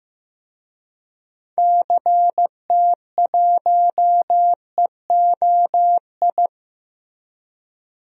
Morse code, heard as a keyed tone, about 15 wpm, 700 Hz. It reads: CT1EOI
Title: DXCC-100 Call Signs - 6 Characters - Repeated 3x - 15wpm